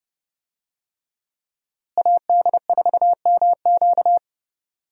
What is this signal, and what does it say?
Morse code, heard as a keyed tone, about 30 wpm, 700 Hz. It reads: AD4MQ
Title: USA Call Signs - 5 Characters - Repeated 3x - 30wpm